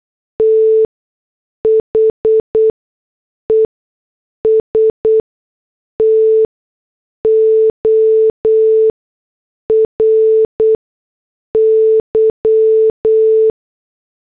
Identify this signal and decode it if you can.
Morse code, heard as a keyed tone, about 8 wpm, spaced as for 6 wpm, 440 Hz. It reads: THE STORY